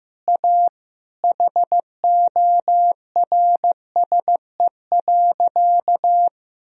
Morse code, heard as a keyed tone, about 15 wpm, 700 Hz. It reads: A HORSE.
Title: Sentences from Top 1000 Words - Review Part 2 - 15wpm